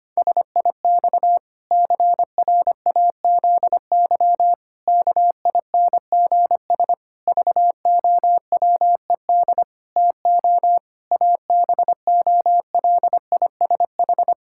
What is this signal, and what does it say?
Morse code, heard as a keyed tone, about 25 wpm, 700 Hz. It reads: SIX CRAZY KINGS VOWED TO ABOLISH
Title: Pangram Challenge v2 - 25wpm